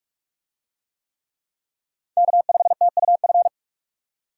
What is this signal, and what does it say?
Morse code, heard as a keyed tone, about 40 wpm, 700 Hz. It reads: K5TUF